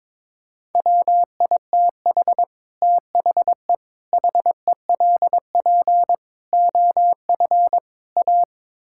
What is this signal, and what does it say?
Morse code, heard as a keyed tone, about 22 wpm, 700 Hz. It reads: WITH THE HELP OF A
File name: practice / 22wpm / Build A Phrase - 5-Word Phrases - 22wpm